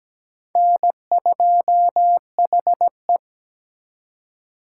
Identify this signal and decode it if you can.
Morse code, heard as a keyed tone, about 20 wpm, 700 Hz. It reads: N2HE
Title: Speed Racing - USA 1x2 Call Signs 17wpm